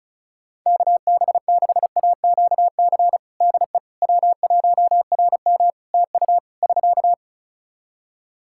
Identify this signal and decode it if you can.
Morse code, heard as a keyed tone, about 36 wpm, 700 Hz. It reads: KB6AQC DE W1RM TU <SK>
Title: Practice QSO - Exchange 4 - Ending - 35wpm